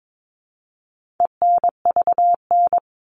Morse code, heard as a keyed tone, about 22 wpm, 700 Hz. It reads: ENVN
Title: All Cut Numbers - 4-Digit 22wpm